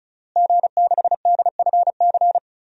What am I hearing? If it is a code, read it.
Morse code, heard as a keyed tone, about 36 wpm, 700 Hz. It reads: G6DFC
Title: DXCC-100 Call Signs - 5 Characters - Repeated 3x - 35wpm